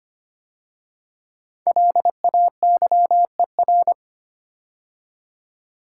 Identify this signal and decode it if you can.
Morse code, heard as a keyed tone, about 25 wpm, 700 Hz. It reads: LAYER
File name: 5 Letter Words - First 3 Letters Given - 25wpm